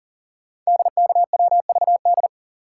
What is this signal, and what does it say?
Morse code, heard as a keyed tone, about 38 wpm, 690 Hz. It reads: DKWVD
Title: Code Groups - 5 Letters - 40wpm